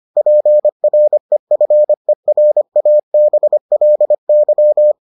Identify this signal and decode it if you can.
Morse code, heard as a keyed tone, about 25 wpm, 600 Hz. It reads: PREFERABLY